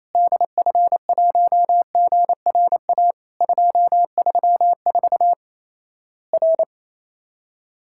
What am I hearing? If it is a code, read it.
Morse code, heard as a keyed tone, about 28 wpm, 700 Hz. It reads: DF1GRA 234 R
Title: Worked All Europe DX Contest - QTC Exchange - 28wpm